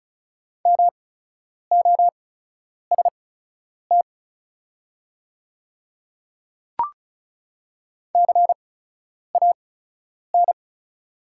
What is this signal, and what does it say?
Morse code, heard as a keyed tone, about 36 wpm, spaced as for 10 wpm, 690 Hz. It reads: MOST CAN
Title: 8x Character Spacing - Top 300 Words - 35wpm